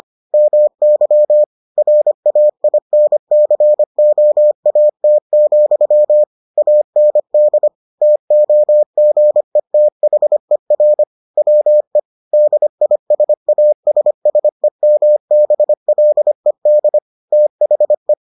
Morse code, heard as a keyed tone, about 25 wpm, 600 Hz. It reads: MY RAINCOAT, AND TOGETHER WE DISASSEMBLED THE